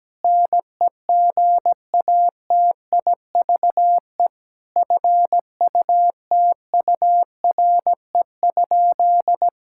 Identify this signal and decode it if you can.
Morse code, heard as a keyed tone, about 17 wpm, 700 Hz. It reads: NEGATIVE FUTURE?
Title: Sentences from Top 1000 Words - Review Part 6 - 17wpm